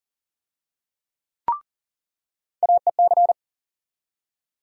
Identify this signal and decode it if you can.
Morse code, heard as a keyed tone, about 40 wpm, 700 Hz. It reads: AEC
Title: USA Call Sign Postfixes - 3 Letters 40wpm